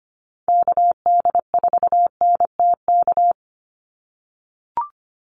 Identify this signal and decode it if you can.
Morse code, heard as a keyed tone, about 25 wpm, 700 Hz. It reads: KD4NTK
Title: USA Call Signs 25wpm